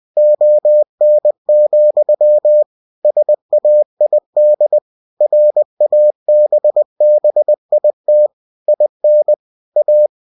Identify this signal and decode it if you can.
Morse code, heard as a keyed tone, about 20 wpm, 600 Hz. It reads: ON, SAID RABBIT IN A